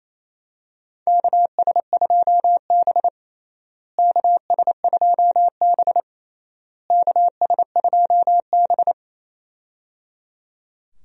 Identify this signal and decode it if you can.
Morse code, heard as a keyed tone, about 28 wpm, 700 Hz. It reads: KS2B KS2B KS2B
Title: USA Call Signs - 4 Characters - Repeated 3x - 28wpm